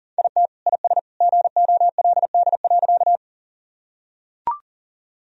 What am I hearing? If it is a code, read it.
Morse code, heard as a keyed tone, about 37 wpm, 700 Hz. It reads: IT IS GOLD.
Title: Sentences from Top 1000 Words - Review Part 3 - 40wpm